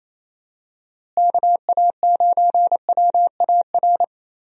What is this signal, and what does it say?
Morse code, heard as a keyed tone, about 27 wpm, 700 Hz. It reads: KA9WAR
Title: USA Call Signs - 6 Characters - Repeated 2x - 28wpm